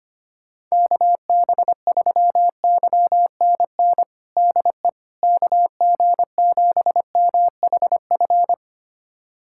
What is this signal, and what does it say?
Morse code, heard as a keyed tone, about 25 wpm, 700 Hz. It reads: KB3YNN DE KG7MHF
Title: USA Callsign DE Callsign - 6 Characters - 25wpm